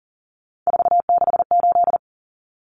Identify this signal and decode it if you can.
Morse code, heard as a keyed tone, about 40 wpm, 700 Hz. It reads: 468
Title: Numbers - 3-Digit 40wpm